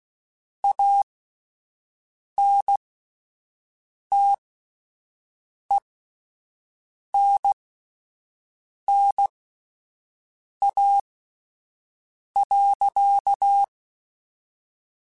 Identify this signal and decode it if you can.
Morse code, heard as a keyed tone, about 16 wpm, spaced as for 5 wpm, 800 Hz. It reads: ANTENNA.
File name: QSOs Spaced